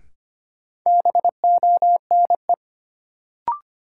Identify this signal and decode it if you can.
Morse code, heard as a keyed tone, about 25 wpm, 700 Hz. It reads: BONE
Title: Sentences from Top 1600 Words - New Word Drill - 25wpm